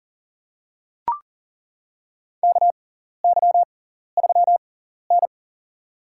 Morse code, heard as a keyed tone, about 40 wpm, spaced as for 14 wpm, 700 Hz. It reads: KY3N
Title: USA Call Signs - 6x Character Spacing - 40wpm